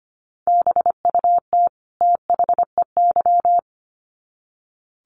Morse code, heard as a keyed tone, about 25 wpm, 700 Hz. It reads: BUT THEY
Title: Sets of 2 Words 25wpm